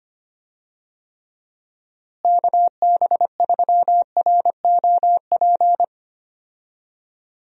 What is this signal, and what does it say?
Morse code, heard as a keyed tone, about 25 wpm, 700 Hz. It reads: KB3ROP
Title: USA Call Signs - 6 Characters - Repeated 3x - 25wpm